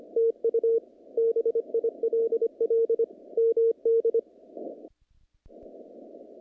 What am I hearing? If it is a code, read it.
Morse code, heard as a keyed tone, about 25 wpm, 460 Hz. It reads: TU BILL MD
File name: contests / naqp-cw-k5tr-op-k5pi- -2200